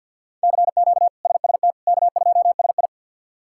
Morse code, heard as a keyed tone, about 50 wpm, 700 Hz. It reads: CQ SST K2SI
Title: K1USN SST QSO Practice - 50wpm